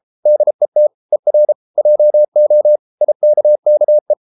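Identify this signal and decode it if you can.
Morse code, heard as a keyed tone, about 33 wpm, 600 Hz. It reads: DET ER JO IKKE
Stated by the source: Morse code at 33 wpm, QRQ.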